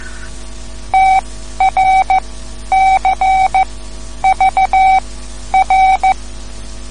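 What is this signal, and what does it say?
Morse code, heard as a keyed tone, about 15 wpm, spaced as for 10 wpm, 760 Hz. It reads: TRCVR
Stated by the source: Morse code lesson, third lesson in a series for learning the Morse alphabet.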